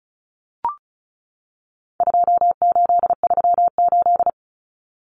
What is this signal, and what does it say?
Morse code, heard as a keyed tone, about 34 wpm, 700 Hz. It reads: 2838
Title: Common Cut Numbers - 4-Digit 35wpm